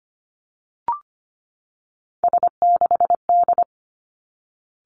Morse code, heard as a keyed tone, about 25 wpm, 700 Hz. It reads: S6D
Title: Three Challenging Letter-Number Combinations 25wpm